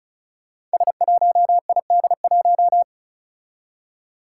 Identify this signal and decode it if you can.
Morse code, heard as a keyed tone, about 36 wpm, 700 Hz. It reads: S1ID1
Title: Five Challenging Letter-Number Combinations 35wpm